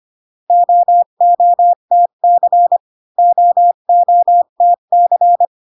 Morse code, heard as a keyed tone, about 25 wpm, 700 Hz. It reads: OOTC OOTC